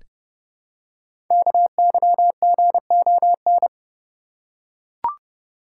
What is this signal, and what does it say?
Morse code, heard as a keyed tone, about 30 wpm, 700 Hz. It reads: KYGON
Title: Code Groups - 5 Letters - 30wpm